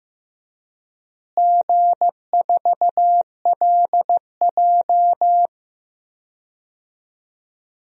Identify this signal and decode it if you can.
Morse code, heard as a keyed tone, about 15 wpm, 700 Hz. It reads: G4LJ
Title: DXCC-100 Call Signs - 4 Characters - Repeated 3x - 15wpm